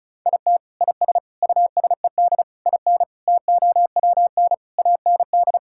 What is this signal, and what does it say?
Morse code, heard as a keyed tone, about 36 wpm, 700 Hz. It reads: IT IS USED IN TOWN AND
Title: Sentences from Top 1600 Words 35wpm